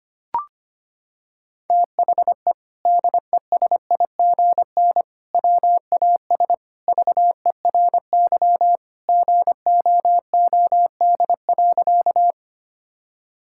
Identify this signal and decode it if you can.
Morse code, heard as a keyed tone, about 25 wpm, 700 Hz. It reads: THE DESIGN WAS VERY GOOD.
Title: Sentences from Top 1000 Words - Review Part 5 - 25wpm